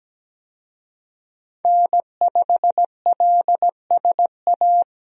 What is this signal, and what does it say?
Morse code, heard as a keyed tone, about 17 wpm, 700 Hz. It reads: N5LSA
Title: USA Call Signs - 5 Characters - 17wpm